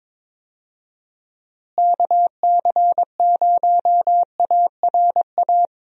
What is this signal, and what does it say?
Morse code, heard as a keyed tone, about 22 wpm, 700 Hz. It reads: KC0ARA
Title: USA Call Signs - 6 Characters - Repeated 2x - 22wpm